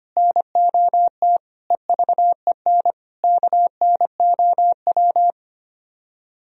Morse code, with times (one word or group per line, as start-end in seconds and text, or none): 0.2-1.4 NOT
1.7-2.9 EVEN
3.2-5.3 KNOW